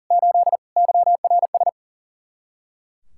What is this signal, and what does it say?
Morse code, heard as a keyed tone, about 47 wpm, 700 Hz. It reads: 8 YRS
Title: Speed-Racing Common QSO Phrases - 40wpm